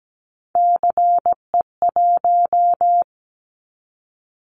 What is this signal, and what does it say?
Morse code, heard as a keyed tone, about 17 wpm, 700 Hz. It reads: CE1
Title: Callsign Prefixes for Top 100 DXCC - 17wpm